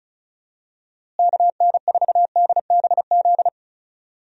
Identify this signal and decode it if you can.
Morse code, heard as a keyed tone, about 36 wpm, 700 Hz. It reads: KN4DBZ
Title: USA Call Signs - 6 Characters - Repeated 2x - 35wpm